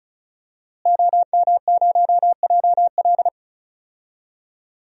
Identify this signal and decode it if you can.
Morse code, heard as a keyed tone, about 36 wpm, 700 Hz. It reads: OM0JL
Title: DXCC-100 Call Signs - 5 Characters - Repeated 3x - 35wpm